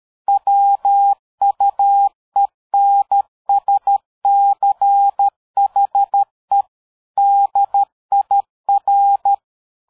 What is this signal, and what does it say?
Morse code, heard as a keyed tone, about 13 wpm, 800 Hz. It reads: WUENSCHE DIR
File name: dk5ke / PersWorte 60